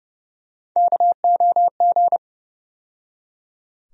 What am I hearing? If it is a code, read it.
Morse code, heard as a keyed tone, about 30 wpm, 700 Hz. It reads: KOG
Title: USA Call Sign Postfixes - 3 Letters 30wpm